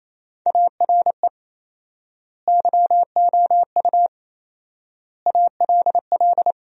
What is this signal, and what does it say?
Morse code, heard as a keyed tone, about 28 wpm, 700 Hz. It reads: ARE YOU ALL
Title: 4x Word-Spacing - Sentences from Top 400 Words - 28wpm